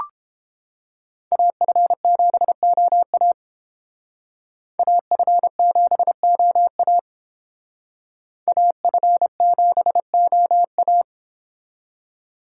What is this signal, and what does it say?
Morse code, heard as a keyed tone, about 30 wpm, 700 Hz. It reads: AF7OA AF7OA AF7OA